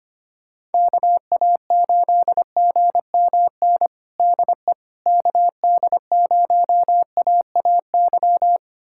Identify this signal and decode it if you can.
Morse code, heard as a keyed tone, about 25 wpm, 700 Hz. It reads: KA8GMN DE KD0AAY
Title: USA Callsign DE Callsign 25wpm